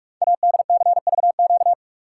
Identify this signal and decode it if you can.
Morse code, heard as a keyed tone, about 44 wpm, 700 Hz. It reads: ADCVQ
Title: Code Groups - 5 Letters - 45wpm